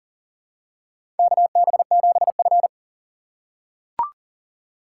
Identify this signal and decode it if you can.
Morse code, heard as a keyed tone, about 40 wpm, 700 Hz. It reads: KB7F
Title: USA Call Signs - 4 Characters - 3x Character Spacing - 40wpm